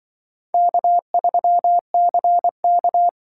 Morse code, heard as a keyed tone, about 24 wpm, 700 Hz. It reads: K3CK
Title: Speed Racing - USA 1x2 Call Signs 20wpm